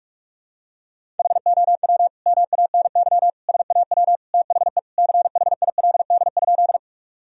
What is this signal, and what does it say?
Morse code, heard as a keyed tone, about 47 wpm, 700 Hz. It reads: HOW MANY SAW THE CHILD?
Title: Sentences from Top 1000 Words - Review Part 1 - 45wpm